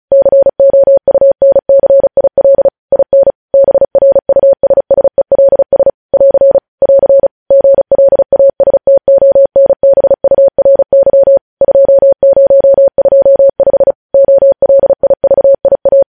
Morse code, heard as a keyed tone, about 36 wpm, 550 Hz. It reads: COUNCIL IN BRUSSELS <AR> <AR> GLASTONBURY 2025 OLIVIA
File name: podcast / News headlines-1x-35x35